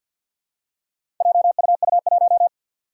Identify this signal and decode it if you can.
Morse code, heard as a keyed tone, about 47 wpm, 700 Hz. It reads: JUU1
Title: Four Challenging Letter-Number Combinations 50wpm